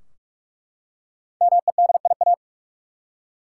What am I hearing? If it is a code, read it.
Morse code, heard as a keyed tone, about 43 wpm, 700 Hz. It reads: MEDIA
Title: Five Letter Words - 45wpm